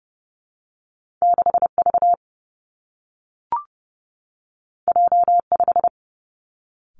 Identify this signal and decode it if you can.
Morse code, heard as a keyed tone, about 30 wpm, 700 Hz. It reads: 6 V J 5
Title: Two Challenging Letter-Number Combinations 30wpm